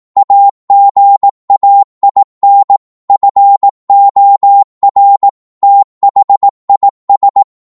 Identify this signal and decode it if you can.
Morse code, heard as a keyed tone, about 18 wpm, 800 Hz. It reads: AGAIN FOR THIS
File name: Morse / psg-18wpm